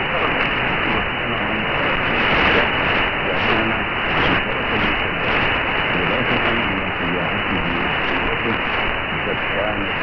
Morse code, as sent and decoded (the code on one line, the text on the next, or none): none